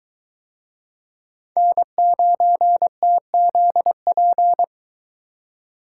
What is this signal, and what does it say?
Morse code, heard as a keyed tone, about 22 wpm, 700 Hz. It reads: N9TZP